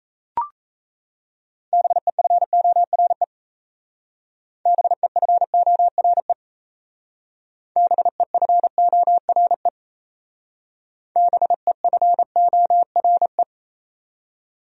Morse code, heard as a keyed tone, about 35 wpm, 700 Hz. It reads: BEFORE BEFORE BEFORE BEFORE